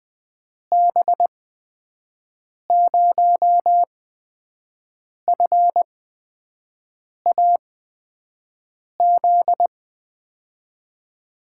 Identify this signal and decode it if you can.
Morse code, heard as a keyed tone, about 20 wpm, spaced as for 5 wpm, 700 Hz. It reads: B0FAZ
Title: USA Call Signs - 8x Character Spacing - 20wpm